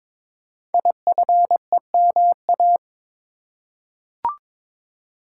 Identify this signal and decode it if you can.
Morse code, heard as a keyed tone, about 22 wpm, 700 Hz. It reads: IFEMA
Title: Code Groups - 5 Letter:Numbers - 22wpm